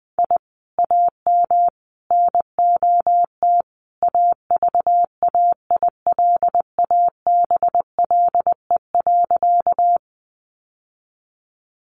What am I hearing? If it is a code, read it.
Morse code, heard as a keyed tone, about 20 wpm, 700 Hz. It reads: I AM NOT AVAILABLE.
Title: Sentences from Top 600 Words 20wpm